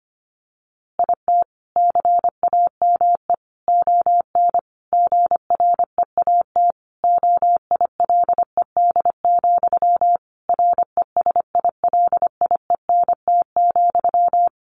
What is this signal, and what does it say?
Morse code, heard as a keyed tone, about 25 wpm, 700 Hz. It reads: IT CAME ON GREAT OILED, RESILIENT,
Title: A Sound Of Thunder 25wpm